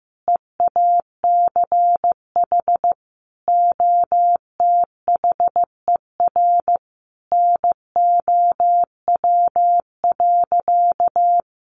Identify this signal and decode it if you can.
Morse code, heard as a keyed tone, about 15 wpm, 700 Hz. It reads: EACH OTHER NOW.
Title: Sentences from Top 1200 Words 15wpm